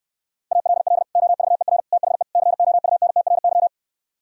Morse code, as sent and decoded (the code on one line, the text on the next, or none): .- .-.. .-.. / --- ...- . .-. / - .... / -.-. --- ..- -. - .-. -.--
ALL OVER THE COUNTRY